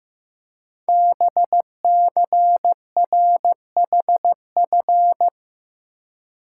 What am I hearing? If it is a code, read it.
Morse code, heard as a keyed tone, about 15 wpm, 700 Hz. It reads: BCRHF
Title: Code Groups - 5 Letters - 15wpm